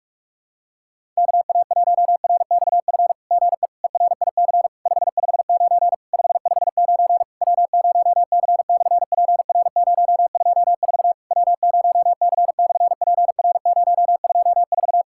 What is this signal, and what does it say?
Morse code, heard as a keyed tone, about 47 wpm, 700 Hz. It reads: KA1RXF GE ERIC 559 559 W0C/PR024 W0C/PR024
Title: SOTA QSO Practice 45wpm